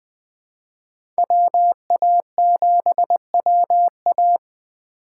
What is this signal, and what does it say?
Morse code, heard as a keyed tone, about 20 wpm, 700 Hz. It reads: WA7WA
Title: USA Call Signs - 5 Characters - 4x Character Spacing - 20wpm